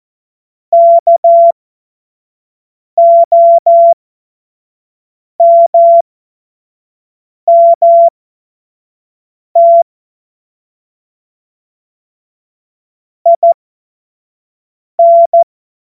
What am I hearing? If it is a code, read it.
Morse code, heard as a keyed tone, about 14 wpm, spaced as for 5 wpm, 680 Hz. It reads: KOMMT IN
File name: Morsekurs / Witze / Witz 04. April 2023 25 F70